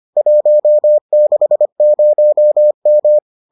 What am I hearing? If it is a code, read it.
Morse code, heard as a keyed tone, about 25 wpm, 600 Hz. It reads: 160M